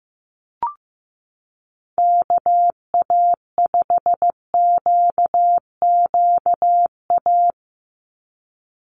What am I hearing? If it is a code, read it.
Morse code, heard as a keyed tone, about 15 wpm, 700 Hz. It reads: KA5QQA
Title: NA SOTA Chaser Callsigns 15wpm